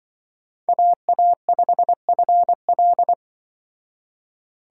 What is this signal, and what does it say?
Morse code, heard as a keyed tone, about 25 wpm, 700 Hz. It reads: AA5FL
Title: Speed Racing - USA 2x2 Call Signs 20wpm